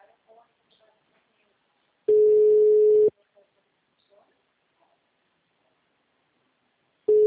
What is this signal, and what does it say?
Ringback tone (Europe, 425 Hz), with speech.